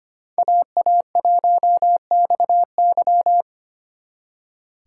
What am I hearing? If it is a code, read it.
Morse code, heard as a keyed tone, about 31 wpm, 700 Hz. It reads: AA1XY
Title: Speed Racing - USA 2x2 Call Signs 25wpm